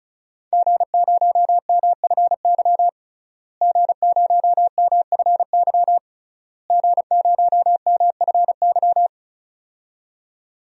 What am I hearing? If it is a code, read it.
Morse code, heard as a keyed tone, about 34 wpm, 700 Hz. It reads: G0MFY G0MFY G0MFY
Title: DXCC-100 Call Signs - 5 Characters - Repeated 3x - 35wpm